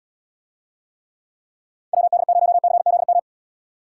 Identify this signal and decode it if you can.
Morse code, heard as a keyed tone, about 75 wpm, 700 Hz. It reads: WD0CZM